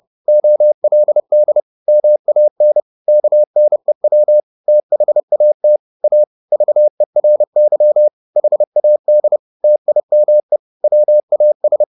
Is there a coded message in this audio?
Yes: Morse code from 0.3 to 11.8 s, heard as a keyed tone, about 30 wpm, 600 Hz. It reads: OLD MAN KNEW THAT A VERY HAD TIME WAS